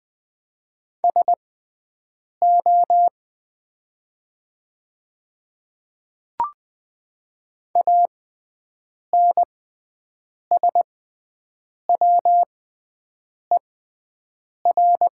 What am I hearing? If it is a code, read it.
Morse code, heard as a keyed tone, about 20 wpm, spaced as for 7 wpm, 700 Hz. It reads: SO ANSWER
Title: 6x Character Spacing - Top 500 Words - 20wpm